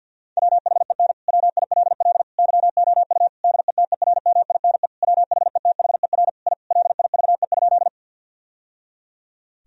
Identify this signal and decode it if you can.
Morse code, heard as a keyed tone, about 50 wpm, 700 Hz. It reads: WHEN WILL YOU DETERMINE WHETHER I LIVE?
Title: Sentences from Top 1000 Words - Review Part 4 - 50wpm